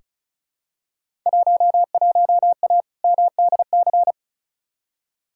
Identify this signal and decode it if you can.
Morse code, heard as a keyed tone, about 36 wpm, 700 Hz. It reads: 11A MDC
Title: Field Day Exchanges Encore - 35wpm